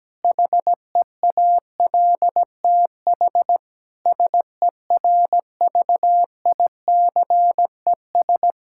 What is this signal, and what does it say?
Morse code, heard as a keyed tone, about 17 wpm, 700 Hz. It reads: HEALTH SERVICES
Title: Practice QSO - Exchange 3 - More Info - 17wpm